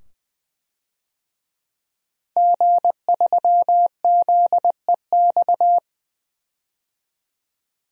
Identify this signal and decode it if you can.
Morse code, heard as a keyed tone, about 20 wpm, 700 Hz. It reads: G3ZEX
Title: DXCC-100 Call Signs - 5 Characters - 20wpm